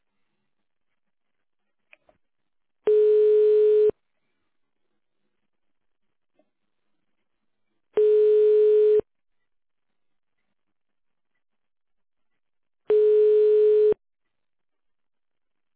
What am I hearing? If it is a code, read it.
Ringback tone (Europe, 425 Hz).